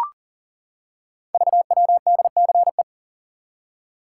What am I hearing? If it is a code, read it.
Morse code, heard as a keyed tone, about 41 wpm, 700 Hz. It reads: VWDCE
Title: Code Groups - 5 Letters - 40wpm